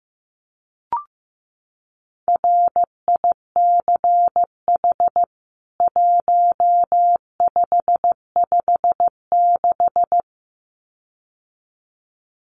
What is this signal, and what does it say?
Morse code, heard as a keyed tone, about 15 wpm, 700 Hz. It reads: RICH 1556
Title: CWT Exchange 15wpm